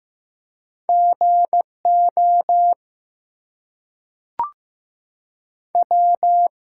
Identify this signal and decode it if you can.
Morse code, heard as a keyed tone, about 15 wpm, 700 Hz. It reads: GO W
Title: Two Letter Words - 15wpm